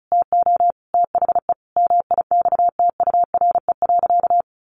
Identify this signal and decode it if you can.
Morse code, heard as a keyed tone, about 34 wpm, 700 Hz. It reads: TO THE MIXTURE.
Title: Sentences from Top 1500 Words 35wpm